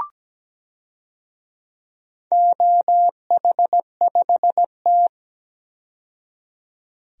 Morse code, heard as a keyed tone, about 17 wpm, 700 Hz. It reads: OH5T
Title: DXCC-100 Call Signs - 4 Characters - 17wpm